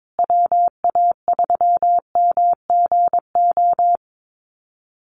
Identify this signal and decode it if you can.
Morse code, heard as a keyed tone, about 22 wpm, 700 Hz. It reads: WA3MGO
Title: USA Call Signs Repeated 3x 22wpm